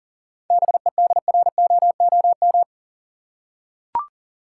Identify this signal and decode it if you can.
Morse code, heard as a keyed tone, about 40 wpm, 700 Hz. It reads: BEDROOM
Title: Seven Letter Words - 40wpm